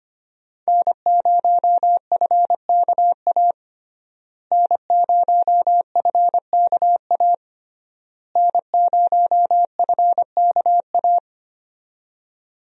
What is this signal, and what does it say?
Morse code, heard as a keyed tone, about 25 wpm, 700 Hz. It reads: N0FKA N0FKA N0FKA
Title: USA Call Signs - 5 Characters - Repeated 3x - 25wpm